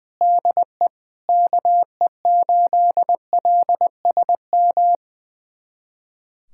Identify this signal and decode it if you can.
Morse code, heard as a keyed tone, about 21 wpm, 700 Hz. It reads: DE KE8LSM
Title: USA Callsign DE Callsign - 6 Characters - 20wpm